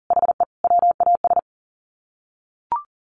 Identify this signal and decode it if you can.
Morse code, heard as a keyed tone, about 37 wpm, 700 Hz. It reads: HE WAS
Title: Sets of 3 Words 40wpm